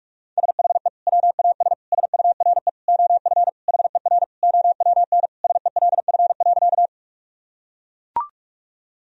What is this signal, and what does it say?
Morse code, heard as a keyed tone, about 42 wpm, 700 Hz. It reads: SHE WAS SURE OF HER OWN SELF.
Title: Sentences from Top 1000 Words - Review Part 1 - 45wpm